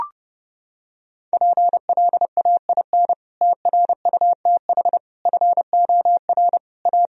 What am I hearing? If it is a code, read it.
Morse code, heard as a keyed tone, about 32 wpm, 700 Hz. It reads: PLAIN TRUTH FOR A
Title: Sentences from Top 1000 Words - Review Part 2 - 30wpm